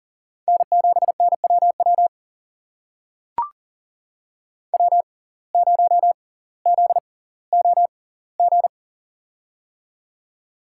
Morse code, heard as a keyed tone, about 40 wpm, spaced as for 20 wpm, 700 Hz. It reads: N7NWW W0ZOG
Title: USA Call Signs - 5 Characters - 6x Character Spacing - 40wpm